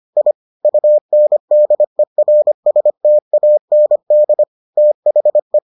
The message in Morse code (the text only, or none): I UNDERSTAND THE